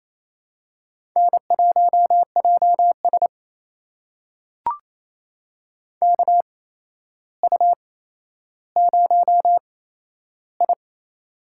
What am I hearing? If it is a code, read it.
Morse code, heard as a keyed tone, about 28 wpm, spaced as for 12 wpm, 700 Hz. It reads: N1JS KU0I
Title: USA Call Signs - 4 Characters - 8x Character Spacing - 28wpm